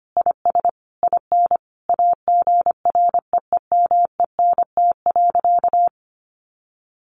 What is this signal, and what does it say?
Morse code, heard as a keyed tone, about 25 wpm, 700 Hz. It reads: IS IN AGREEMENT.
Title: Sentences from Top 1500 Words 25wpm